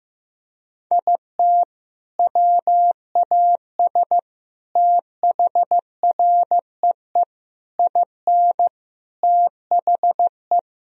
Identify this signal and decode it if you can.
Morse code, heard as a keyed tone, about 15 wpm, 700 Hz. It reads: IT WAS THREE IN THE
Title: Sentences from Top 1000 Words - Review Part 1 - 15wpm